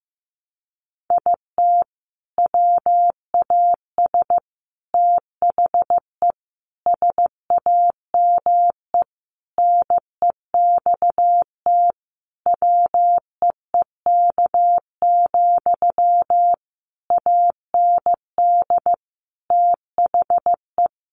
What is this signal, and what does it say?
Morse code, heard as a keyed tone, about 15 wpm, 700 Hz. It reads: IT WAS THE SAME NEXT WEEK, AND THE
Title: Sentences from Top 500 Words 15wpm